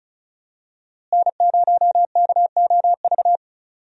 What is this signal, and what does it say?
Morse code, heard as a keyed tone, about 36 wpm, 700 Hz. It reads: N0KOV
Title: USA Call Signs - 5 Characters - Repeated 3x - 35wpm